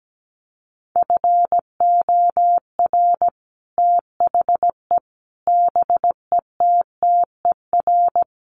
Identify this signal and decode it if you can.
Morse code, heard as a keyed tone, about 17 wpm, 700 Hz. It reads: FOR THE BETTER
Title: Sets of 5 Words Encore2 17wpm